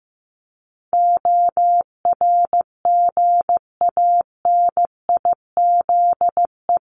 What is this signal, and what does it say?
Morse code, heard as a keyed tone, about 15 wpm, 700 Hz. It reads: ORGANIZE
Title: Top 1751-2000 Words 15wpm